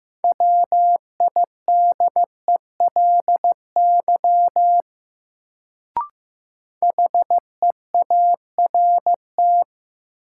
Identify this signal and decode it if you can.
Morse code, heard as a keyed tone, about 15 wpm, 700 Hz. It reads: WIDELY HEART
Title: Sentences from Top 1200 Words - New Word Drill - 15wpm